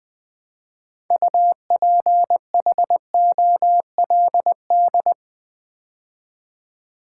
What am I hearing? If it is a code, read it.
Morse code, heard as a keyed tone, about 20 wpm, 700 Hz. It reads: UPHOLD
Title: Build A Word - 3 Letters At A Time - 6-Letter Words - 20wpm